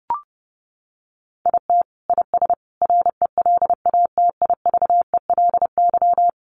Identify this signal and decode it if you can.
Morse code, heard as a keyed tone, about 30 wpm, 700 Hz. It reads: IT IS RELATIVELY
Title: Sentences from Top 1100 Words 30wpm